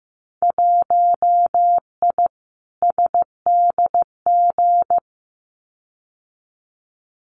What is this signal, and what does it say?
Morse code, heard as a keyed tone, about 15 wpm, 700 Hz. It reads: 1I SDG
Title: Winter Field Day Exchanges 15wpm